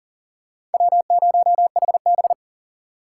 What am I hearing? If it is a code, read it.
Morse code, heard as a keyed tone, about 40 wpm, 700 Hz. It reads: W0HB
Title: USA Call Signs - 4 Characters - 3x Character Spacing - 40wpm